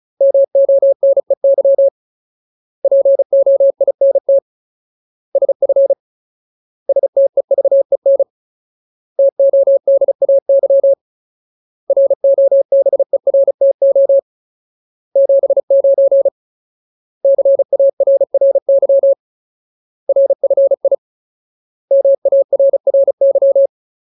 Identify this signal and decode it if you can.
Morse code, heard as a keyed tone, about 35 wpm, 550 Hz. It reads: MONEY POINT SF STEVEN TODAY ROBERTO 79 CARRY RFI MARRY
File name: Ultimate Word List / 35WPM Ultimate